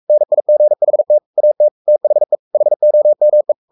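Morse code, heard as a keyed tone, about 44 wpm, 600 Hz. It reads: NIGHT AT THE HOME
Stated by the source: Morse code at 43 wpm sending Ken Follett's novel Never in English, from CW med Gnister.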